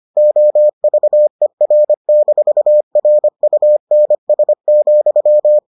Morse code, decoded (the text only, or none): OVER-RUNS,